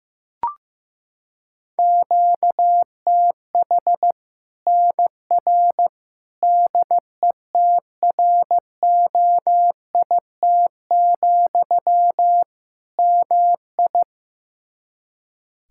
Morse code, heard as a keyed tone, about 15 wpm, 700 Hz. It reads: QTH NR DETROIT, MI
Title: Practice QSO - Exchange 1 - Beginning - 15wpm